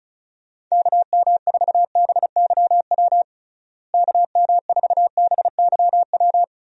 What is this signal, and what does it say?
Morse code, heard as a keyed tone, about 34 wpm, 700 Hz. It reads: KM4BYW KM4BYW
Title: USA Call Signs - 6 Characters - Repeated 3x - 35wpm